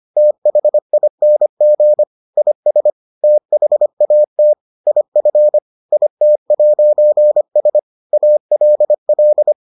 Morse code, heard as a keyed tone, about 25 wpm, 600 Hz. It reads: THING IS THAT IF IT'S ALL